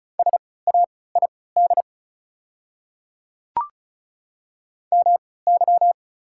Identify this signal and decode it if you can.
Morse code, heard as a keyed tone, about 34 wpm, spaced as for 20 wpm, 690 Hz. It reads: SAID MY
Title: 3x Character Spacing - Top 100 Words - 35wpm